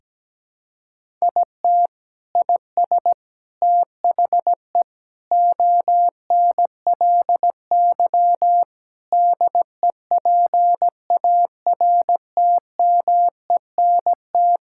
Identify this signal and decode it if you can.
Morse code, heard as a keyed tone, about 17 wpm, 700 Hz. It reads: IT IS THE ONLY DEPARTMENT
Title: Sentences from Top 1000 Words - Review Part 5 - 17wpm